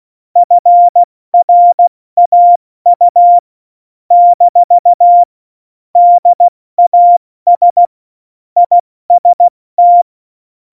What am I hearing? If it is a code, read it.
Morse code, heard as a keyed tone, about 16 wpm, 700 Hz. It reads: FRAU - DAS IST
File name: Morsekurs / Witze / Witz 23. April 2019, 70 F80